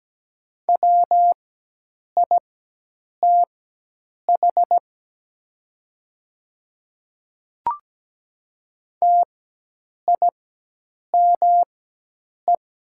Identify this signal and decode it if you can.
Morse code, heard as a keyed tone, about 17 wpm, spaced as for 8 wpm, 700 Hz. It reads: WITH TIME
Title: 4x Character Spacing - Top 200 Words - 17wpm